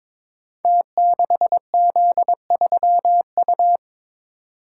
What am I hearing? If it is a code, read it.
Morse code, heard as a keyed tone, about 22 wpm, 690 Hz. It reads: T6Z3U
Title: Code Groups - 5 Letter:Numbers - 22wpm